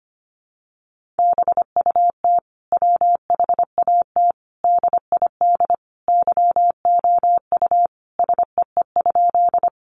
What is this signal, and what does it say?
Morse code, heard as a keyed tone, about 25 wpm, 700 Hz. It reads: BUT WHAT DID YOU SEE?
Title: Sentences from Top 100 Words 25wpm